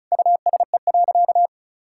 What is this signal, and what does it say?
Morse code, heard as a keyed tone, about 34 wpm, 700 Hz. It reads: USE.
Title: Sentences from Top 1000 Words - Review Part 6 - 35wpm